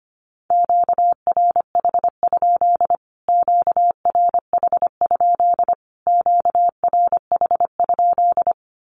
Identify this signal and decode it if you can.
Morse code, heard as a keyed tone, about 25 wpm, 700 Hz. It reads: QRH? QRH? QRH?
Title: Q-Codes Repeated 3x 25wpm